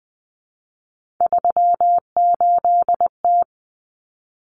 Morse code, heard as a keyed tone, about 20 wpm, 700 Hz. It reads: 38T
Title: Common Cut Numbers - 3-Digit 20wpm